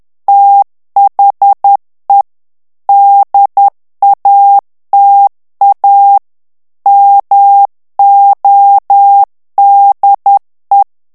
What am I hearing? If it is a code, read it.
Morse code, heard as a keyed tone, about 10 wpm, 800 Hz. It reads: THE DATA MODE
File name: ANGLAIS texte clair 3 vitesse 10 mots minute